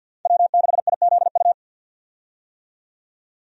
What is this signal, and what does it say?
Morse code, heard as a keyed tone, about 46 wpm, 700 Hz. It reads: W6IZU
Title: USA Call Signs - 5 Characters - 50wpm